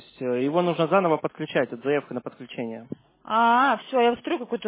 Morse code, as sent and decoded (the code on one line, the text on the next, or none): none